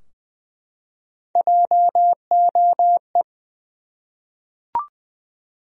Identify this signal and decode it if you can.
Morse code, heard as a keyed tone, about 20 wpm, 700 Hz. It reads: JOE
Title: USA Common Names - Max 5 Letters - 20wpm